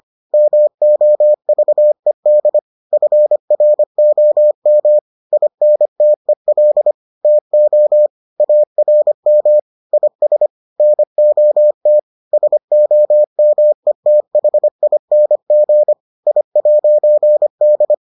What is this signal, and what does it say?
Morse code, heard as a keyed tone, about 25 wpm, 600 Hz. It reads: MOVED FROM INTEL TO ARM IS NOT SOMETHING I'D